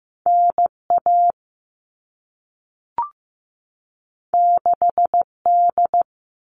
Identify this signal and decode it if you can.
Morse code, heard as a keyed tone, about 15 wpm, 700 Hz. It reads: NA 6D
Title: All Cut Numbers - 2-Digit 15wpm